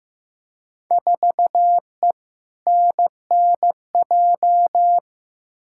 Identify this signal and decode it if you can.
Morse code, heard as a keyed tone, about 15 wpm, 700 Hz. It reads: 4E NNJ
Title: Field Day Exchanges - 15wpm